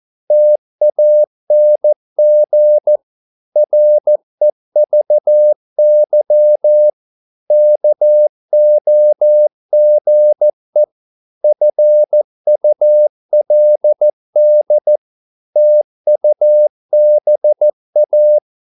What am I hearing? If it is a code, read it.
Morse code, heard as a keyed tone, about 14 wpm, 600 Hz. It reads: TANG REVY KOGE FULD TUBA
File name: Korte ord DK 14wpm